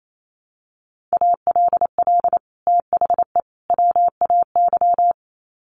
Morse code, heard as a keyed tone, about 28 wpm, 700 Hz. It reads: ALL THE WAY
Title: Sets of 3 Words 28wpm